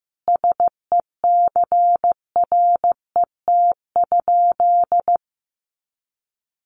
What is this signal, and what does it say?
Morse code, heard as a keyed tone, about 15 wpm, 700 Hz. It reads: SECRET?
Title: Sentences from Top 1200 Words 15wpm